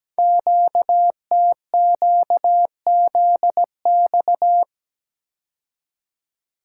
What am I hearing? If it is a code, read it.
Morse code, heard as a keyed tone, about 17 wpm, 700 Hz. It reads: QTQZX
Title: Code Groups - 5 Letters - 17wpm